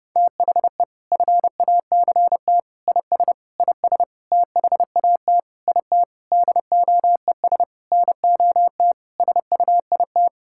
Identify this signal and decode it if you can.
Morse code, heard as a keyed tone, about 31 wpm, 700 Hz. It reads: THE FACT IS IS THAT IT DOES NOT SUIT
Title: Sentences from Top 1600 Words 30wpm